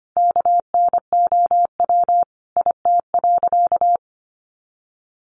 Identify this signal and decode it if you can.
Morse code, heard as a keyed tone, about 25 wpm, 700 Hz. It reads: KNOW IT.
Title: Sentences from Top 900 Words 25wpm